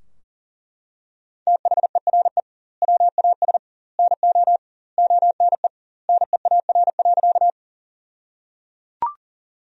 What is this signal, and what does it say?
Morse code, heard as a keyed tone, about 37 wpm, 700 Hz. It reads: THERE WAS NO ONE NEAR.
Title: 2x Word-Spacing - Sentences from Top 200 Words - 40wpm